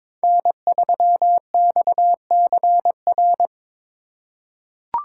Morse code, heard as a keyed tone, about 22 wpm, 700 Hz. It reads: N3XCR
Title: USA Call Signs - 5 Characters - 4x Character Spacing - 22wpm